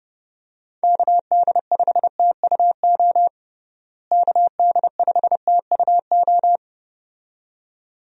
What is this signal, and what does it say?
Morse code, heard as a keyed tone, about 30 wpm, 700 Hz. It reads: KD5TUO KD5TUO
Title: USA Call Signs - 6 Characters - Repeated 2x - 30wpm